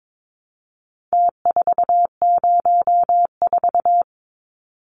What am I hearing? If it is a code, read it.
Morse code, heard as a keyed tone, about 22 wpm, 700 Hz. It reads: T404
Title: Numbers - Contest Style - 4-Digit 22wpm